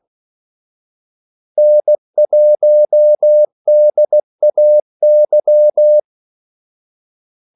Morse code, heard as a keyed tone, about 16 wpm, 600 Hz. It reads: N1DAY
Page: Kaldesignaler 16 wpm